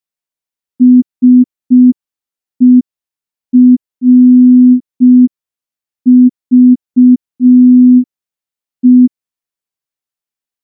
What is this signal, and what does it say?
Morse code, heard as a keyed tone, about 5 wpm, 250 Hz. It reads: SERVE
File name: morse / morse-training-05WPM